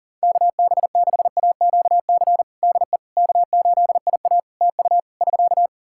Morse code, heard as a keyed tone, about 40 wpm, 700 Hz. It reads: KB6AQC DE K8IA TU <SK>
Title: K1USN SST QSO Practice 40wpm